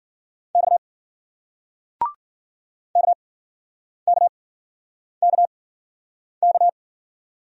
Morse code, heard as a keyed tone, about 53 wpm, spaced as for 9 wpm, 700 Hz. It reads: X KKKK